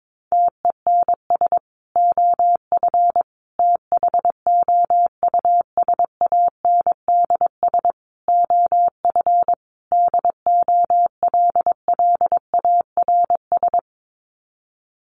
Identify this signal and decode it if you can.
Morse code, heard as a keyed tone, about 22 wpm, 700 Hz. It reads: TENS OF THOUSANDS OF DOLLARS
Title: Sets of 5 Words Encore2 22wpm